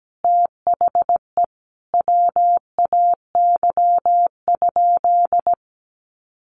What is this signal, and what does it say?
Morse code, heard as a keyed tone, about 17 wpm, 700 Hz. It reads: THE WAY?
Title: Sentences from Top 1100 Words 17wpm